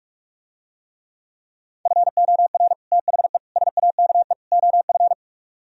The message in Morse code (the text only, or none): FOR THE SAKE OF